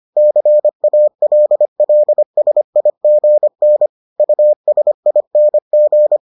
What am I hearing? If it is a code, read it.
Morse code, heard as a keyed tone, about 25 wpm, 600 Hz. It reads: CALLSIGN USING